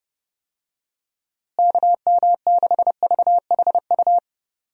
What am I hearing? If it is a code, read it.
Morse code, heard as a keyed tone, about 32 wpm, 700 Hz. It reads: KM6VHU
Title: USA Call Signs - 6 Characters - Repeated 3x - 30wpm